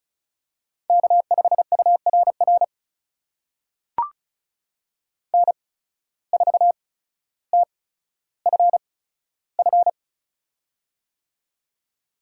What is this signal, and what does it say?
Morse code, heard as a keyed tone, about 36 wpm, spaced as for 15 wpm, 700 Hz. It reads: K5URR N4TFF
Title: USA Call Signs - 5 Characters - 8x Character Spacing - 35wpm